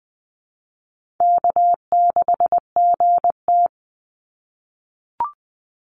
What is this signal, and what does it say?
Morse code, heard as a keyed tone, about 20 wpm, 700 Hz. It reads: K6GT
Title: Call Signs - 20wpm